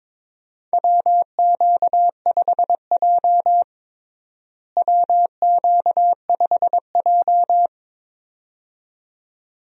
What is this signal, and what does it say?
Morse code, heard as a keyed tone, about 22 wpm, 700 Hz. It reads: WQ5J WQ5J
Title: USA Call Signs - 4 Characters - Repeated 2x - 22wpm